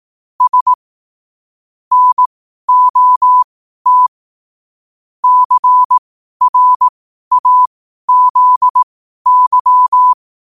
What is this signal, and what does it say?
Morse code, heard as a keyed tone, about 18 wpm, spaced as for 12 wpm, 1000 Hz. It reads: S NOT CRAZY
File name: qotdfiles / 12wpm